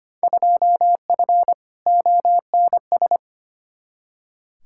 Morse code, heard as a keyed tone, about 25 wpm, 700 Hz. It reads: 2F ONS
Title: Field Day Exchanges - 25wpm